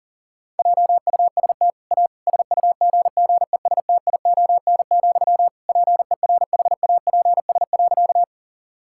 Morse code, heard as a keyed tone, about 40 wpm, 700 Hz. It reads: JUST A SUGGESTION, PERHAPS.
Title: Sentences from Top 1600 Words 40wpm